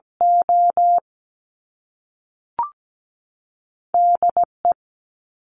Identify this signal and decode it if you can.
Morse code, heard as a keyed tone, about 17 wpm, 700 Hz. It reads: O DE
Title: US States 17wpm